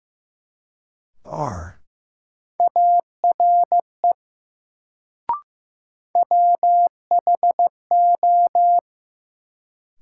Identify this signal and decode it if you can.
Morse code, heard as a keyed tone, about 15 wpm, 700 Hz. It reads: ARE WHO
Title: Sentences from Top 100 Words - New Word Drill - 15wpm